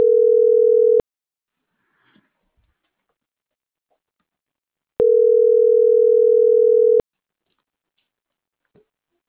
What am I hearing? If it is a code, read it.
Ringback tone (North America), 440 + 480 Hz.